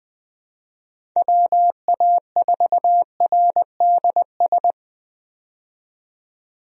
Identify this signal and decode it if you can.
Morse code, heard as a keyed tone, about 20 wpm, 700 Hz. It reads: WA4RDS
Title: USA Call Signs - 6 Characters - 20wpm